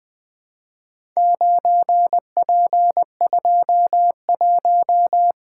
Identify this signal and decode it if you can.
Morse code, heard as a keyed tone, about 20 wpm, 700 Hz. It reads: 9P21
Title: Four Letter-Number Combinations - 20wpm